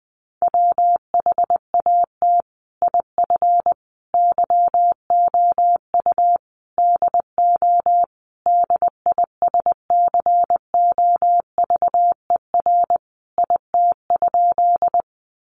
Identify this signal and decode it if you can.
Morse code, heard as a keyed tone, about 20 wpm, 700 Hz. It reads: WHAT IF YOU DO DISCOVER IT?
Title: Sentences from Top 1100 Words 20wpm